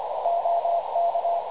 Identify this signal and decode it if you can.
Morse code, heard as a keyed tone, about 25 wpm, 680 Hz, with radio noise. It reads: OK
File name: dlazaawansowanych / nagraniazpasma / websdr recording 3567.0kHz